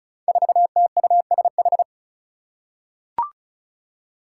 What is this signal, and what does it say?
Morse code, heard as a keyed tone, about 36 wpm, 700 Hz. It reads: 4TUSH
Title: Five Challenging Letter-Number Combinations 35wpm